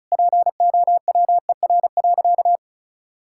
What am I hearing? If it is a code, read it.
Morse code, heard as a keyed tone, about 36 wpm, 700 Hz. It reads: POWER.
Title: Sentences from Top 1000 Words - Review Part 5 - 35wpm